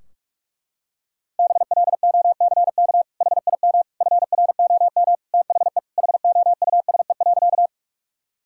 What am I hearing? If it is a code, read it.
Morse code, heard as a keyed tone, about 41 wpm, 700 Hz. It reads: BLOCK HIM FROM THE HOUSE.
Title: Sentences from Top 1600 Words 45wpm